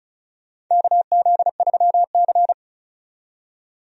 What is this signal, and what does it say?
Morse code, heard as a keyed tone, about 35 wpm, 700 Hz. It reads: KZ3C
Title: USA Call Signs - 4 Characters - 35wpm